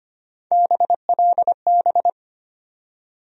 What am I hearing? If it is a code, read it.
Morse code, heard as a keyed tone, about 25 wpm, 700 Hz. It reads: BLB
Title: USA Call Sign Postfixes - 3 Letters 25wpm